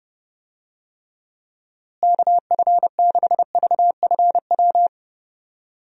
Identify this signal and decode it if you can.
Morse code, heard as a keyed tone, about 30 wpm, 700 Hz. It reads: KF6VFW
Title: USA Call Signs - 6 Characters - Repeated 3x - 30wpm